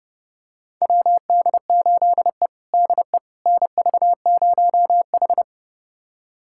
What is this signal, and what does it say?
Morse code, heard as a keyed tone, about 30 wpm, 700 Hz. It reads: WD8E DE NV0H
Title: USA Callsign DE Callsign - 4 Characters - 30wpm